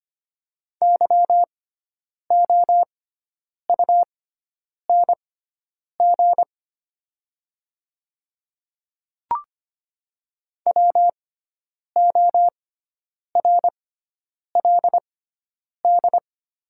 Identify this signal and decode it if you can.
Morse code, heard as a keyed tone, about 25 wpm, spaced as for 9 wpm, 700 Hz. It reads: YOUNG WORLD
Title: 6x Character Spacing - Top 500 Words - 25wpm